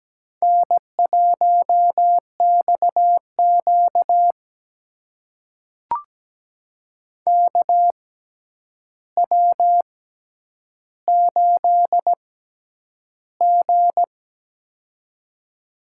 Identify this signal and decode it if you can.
Morse code, heard as a keyed tone, about 17 wpm, spaced as for 9 wpm, 700 Hz. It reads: N1XQ KW8G
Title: USA Call Signs - 4 Characters - 6x Character Spacing - 17wpm